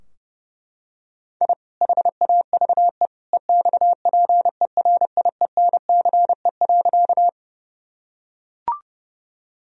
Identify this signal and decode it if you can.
Morse code, heard as a keyed tone, about 32 wpm, 700 Hz. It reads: I HAVE EXPERIENCE.
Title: Sentences from Top 1000 Words - Review Part 3 - 30wpm